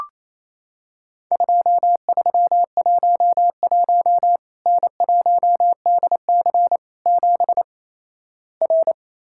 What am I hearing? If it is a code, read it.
Morse code, heard as a keyed tone, about 28 wpm, 700 Hz. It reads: 2311 N1DC 7 R
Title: Worked All Europe DX Contest - QTC Exchange - 28wpm